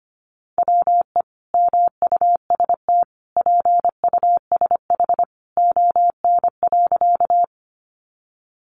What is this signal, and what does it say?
Morse code, heard as a keyed tone, about 25 wpm, 700 Hz. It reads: WE MUST PUSH ON.
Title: Sentences from Top 1300 Words 25wpm